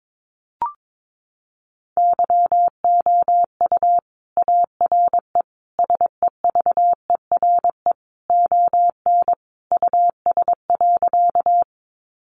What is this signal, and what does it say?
Morse code, heard as a keyed tone, about 22 wpm, 700 Hz. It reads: YOU ARE SEVERE ON US.
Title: Sentences from Top 1400 Words 22wpm